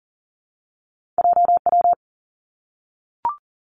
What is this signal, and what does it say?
Morse code, heard as a keyed tone, about 38 wpm, 700 Hz. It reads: J W
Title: Two Challenging Letter-Number Combinations 40wpm